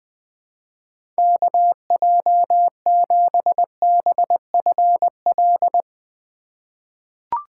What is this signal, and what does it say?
Morse code, heard as a keyed tone, about 20 wpm, 700 Hz. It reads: KJ7BFL
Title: USA Call Signs - 6 Characters - 4x Character Spacing - 20wpm